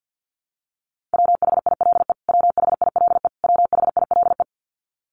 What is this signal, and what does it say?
Morse code, heard as a keyed tone, about 50 wpm, 700 Hz. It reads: WHILE WHILE WHILE
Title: Top 500 Words Repeated 50wpm